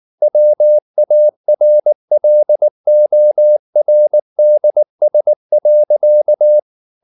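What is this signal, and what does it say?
Morse code, heard as a keyed tone, about 19 wpm, 600 Hz. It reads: WARLORDS.
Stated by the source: Morse code at 19 wpm from CW med Gnister, reading Ken Follett's novel Never in English.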